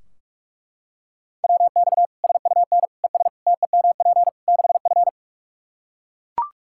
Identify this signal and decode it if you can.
Morse code, heard as a keyed tone, about 41 wpm, 700 Hz. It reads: WX SUN ES TEMP 6F
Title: Practice QSO - Exchange 2 - Info - 45wpm